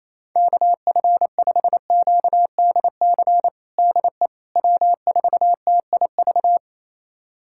Morse code, heard as a keyed tone, about 28 wpm, 700 Hz. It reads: KF5QDC DE W4TIV
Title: USA Callsign DE Callsign 28wpm